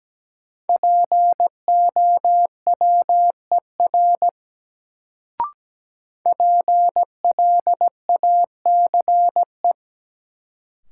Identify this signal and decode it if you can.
Morse code, heard as a keyed tone, about 17 wpm, 700 Hz. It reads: POWER PLACE
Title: Sets of 4 Words - New Word Drill - 17wpm